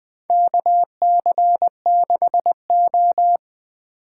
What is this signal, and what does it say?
Morse code, heard as a keyed tone, about 20 wpm, 700 Hz. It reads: KC6O
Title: Four Letter-Number Combinations - 20wpm